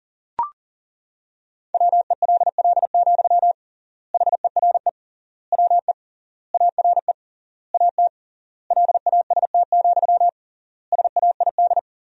Morse code, heard as a keyed tone, about 37 wpm, 700 Hz. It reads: WELL, HERE WE ARE AT LAST, SAID
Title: 3x Word-Spacing - Sentences from Top 300 Words - 40wpm